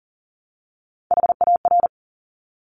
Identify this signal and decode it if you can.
Morse code, heard as a keyed tone, about 38 wpm, 690 Hz. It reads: HAR
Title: Top Three Letter Combinations in Words 40wpm